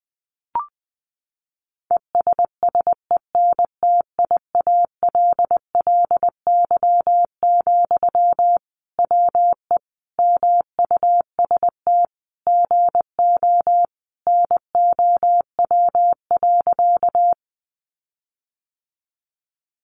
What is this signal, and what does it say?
Morse code, heard as a keyed tone, about 20 wpm, 700 Hz. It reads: ESSENTIALLY, WE MUST GO NOW.
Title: Sentences from Top 1300 Words 20wpm